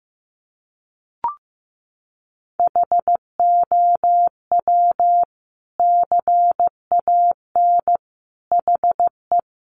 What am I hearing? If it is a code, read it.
Morse code, heard as a keyed tone, about 15 wpm, 700 Hz. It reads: HOW CAN HE
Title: Sentences from Top 1100 Words 15wpm